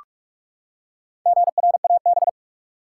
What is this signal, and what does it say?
Morse code, heard as a keyed tone, about 46 wpm, 700 Hz. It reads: GRAB
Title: Four Letter Words - 45wpm